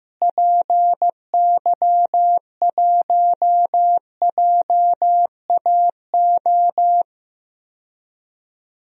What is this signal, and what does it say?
Morse code, heard as a keyed tone, about 15 wpm, 700 Hz. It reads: PY1JAO
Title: DXCC-100 Call Signs - 6 Characters - Repeated 2x - 15wpm